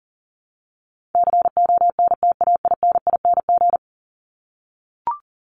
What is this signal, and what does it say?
Morse code, heard as a keyed tone, about 40 wpm, 700 Hz. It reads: CONTAINING
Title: ING Suffix - Top 100 Words with Base Words 40wpm